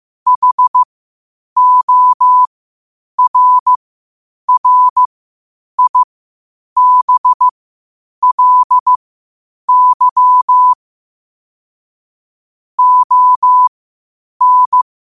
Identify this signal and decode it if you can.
Morse code, heard as a keyed tone, about 15 wpm, spaced as for 8 wpm, 1000 Hz. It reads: HORRIBLY ON
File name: qotdfiles / 8wpm